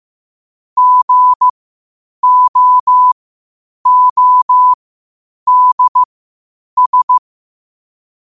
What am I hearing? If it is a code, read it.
Morse code, heard as a keyed tone, about 15 wpm, spaced as for 8 wpm, 1000 Hz. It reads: GOODS
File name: qotdfiles / 8wpm